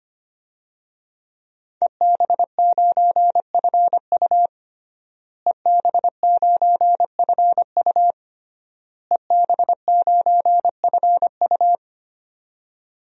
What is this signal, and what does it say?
Morse code, heard as a keyed tone, about 25 wpm, 700 Hz. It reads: EB9FU EB9FU EB9FU
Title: DXCC-100 Call Signs - 5 Characters - Repeated 3x - 25wpm